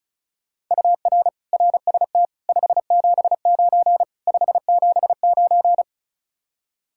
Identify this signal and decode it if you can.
Morse code, heard as a keyed tone, about 45 wpm, 700 Hz. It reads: UR RST 579 579